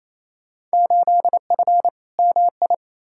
Morse code, heard as a keyed tone, about 28 wpm, 700 Hz. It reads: 8F MI
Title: Field Day Exchanges Encore - 28wpm